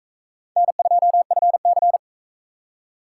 Morse code, heard as a keyed tone, about 44 wpm, 700 Hz. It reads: N2FC